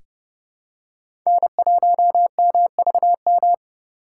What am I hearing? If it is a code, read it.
Morse code, heard as a keyed tone, about 30 wpm, 700 Hz. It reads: N1MVM
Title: USA Call Signs - 5 Characters - 2x Character Spacing - 30wpm